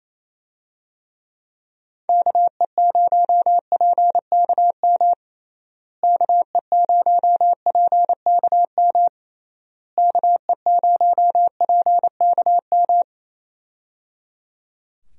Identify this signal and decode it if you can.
Morse code, heard as a keyed tone, about 28 wpm, 700 Hz. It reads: KE0PKM KE0PKM KE0PKM
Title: USA Call Signs - 6 Characters - Repeated 3x - 28wpm